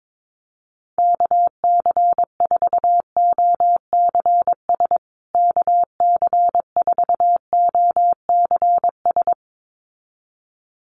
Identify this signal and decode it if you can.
Morse code, heard as a keyed tone, about 22 wpm, 700 Hz. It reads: KC4OCS KC4OCS
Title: USA Call Signs Repeated 2x 22wpm